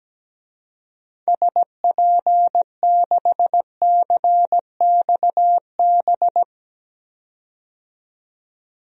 Morse code, heard as a keyed tone, about 17 wpm, 700 Hz. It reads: SP6CXB
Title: DXCC-100 Call Signs - 6 Characters - 17wpm